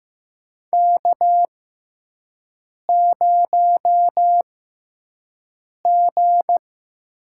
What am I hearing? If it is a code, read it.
Morse code, heard as a keyed tone, about 15 wpm, spaced as for 5 wpm, 700 Hz. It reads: K0G
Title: USA Call Signs - 4 Characters - 6x Character Spacing - 15wpm